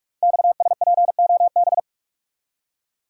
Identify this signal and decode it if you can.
Morse code, heard as a keyed tone, about 44 wpm, 700 Hz. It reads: XSPOB
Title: Code Groups - 5 Letters - 45wpm